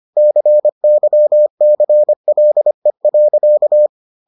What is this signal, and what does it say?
Morse code, heard as a keyed tone, about 25 wpm, 600 Hz. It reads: CYCLE.